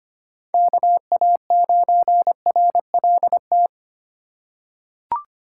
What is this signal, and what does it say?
Morse code, heard as a keyed tone, about 25 wpm, 700 Hz. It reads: KA9RLT
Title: USA Call Signs - 3x Character Spacing - 25wpm